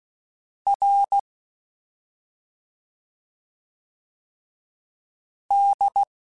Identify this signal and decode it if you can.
Morse code, heard as a keyed tone, about 16 wpm, 800 Hz. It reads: R D
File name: Special INGs / Special INGs-mp3s / SlowToFaster ing @5